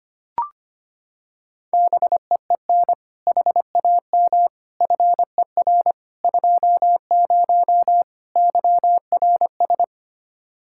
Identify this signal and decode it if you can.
Morse code, heard as a keyed tone, about 25 wpm, 700 Hz. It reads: BEEN HAM FER 20 YRS
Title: K1USN SST QSO Practice 25wpm